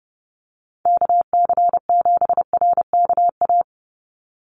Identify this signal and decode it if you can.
Morse code, heard as a keyed tone, about 30 wpm, 700 Hz. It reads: KC7RKA
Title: USA Call Signs 30wpm